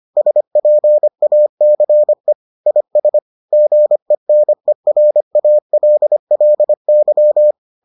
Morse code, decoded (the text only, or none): SPACE IS GENERALLY